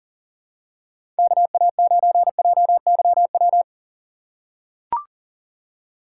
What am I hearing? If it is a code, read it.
Morse code, heard as a keyed tone, about 40 wpm, 700 Hz. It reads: KA9JYW
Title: USA Call Signs - 6 Characters - 3x Character Spacing - 40wpm